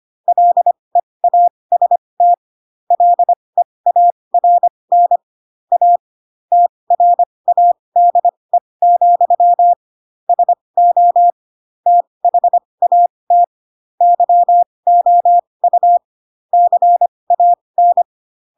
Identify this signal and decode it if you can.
Morse code, heard as a keyed tone, about 25 wpm, spaced as for 20 wpm, 700 Hz. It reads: LEAST LEARN A TRADE, SO THAT YOU CAN
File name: Pinocchio / Pinocchio ch4 15 25 ik1hge